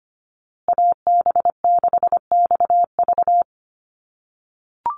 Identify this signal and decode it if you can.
Morse code, heard as a keyed tone, about 25 wpm, 700 Hz. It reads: AB6XV
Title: USA Call Signs Repeated 2x 25wpm